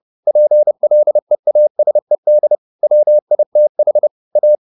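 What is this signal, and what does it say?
Morse code, heard as a keyed tone, about 30 wpm, 600 Hz. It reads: PLEASED WITH A
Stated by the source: Morse code at 30 wpm, the novel Never by Ken Follett in English, from CW med Gnister.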